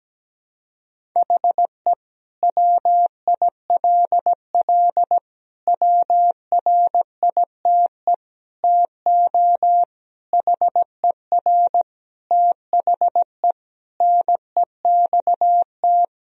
Morse code, heard as a keyed tone, about 17 wpm, 700 Hz. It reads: HE WILL WRITE TO HER THE NEXT
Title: Sentences from Top 1000 Words - Review Part 1 - 17wpm